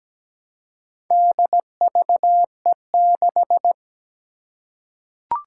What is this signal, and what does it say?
Morse code, heard as a keyed tone, about 17 wpm, 700 Hz. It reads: DVE6
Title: Four Challenging Letter-Number Combinations 17wpm